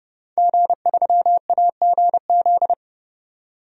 Morse code, heard as a keyed tone, about 30 wpm, 700 Hz. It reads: G3AGZ
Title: DXCC-100 Call Signs - 5 Characters - 30wpm